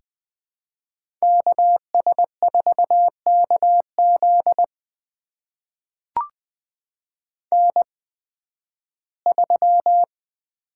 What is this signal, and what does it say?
Morse code, heard as a keyed tone, about 20 wpm, spaced as for 13 wpm, 700 Hz. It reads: KS4KZ N3
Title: USA Call Signs - 5 Characters - 8x Character Spacing - 20wpm